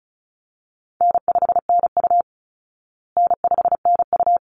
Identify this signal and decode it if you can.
Morse code, heard as a keyed tone, about 34 wpm, 700 Hz. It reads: N5NU N5NU
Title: Easy Call Signs Repeated 2x - 35wpm